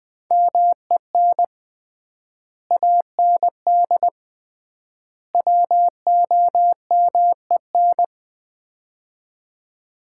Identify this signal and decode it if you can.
Morse code, heard as a keyed tone, about 21 wpm, 700 Hz. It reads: MEN AND WOMEN
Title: 3x Word-Spacing - Sets of 3 Words - 20wpm